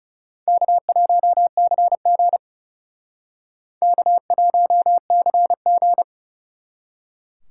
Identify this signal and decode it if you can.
Morse code, heard as a keyed tone, about 38 wpm, 700 Hz. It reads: K1CG K1CG